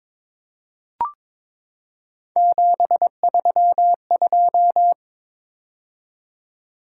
Morse code, heard as a keyed tone, about 22 wpm, 700 Hz. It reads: 732
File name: practice / 22wpm / Three Letter-Number Combinations - 22wpm